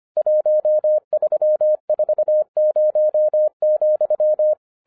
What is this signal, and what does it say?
Morse code, heard as a keyed tone, about 25 wpm, 610 Hz. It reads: 1340,
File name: 2011-07-mp3 / n24 QRP OTAKU 25wpm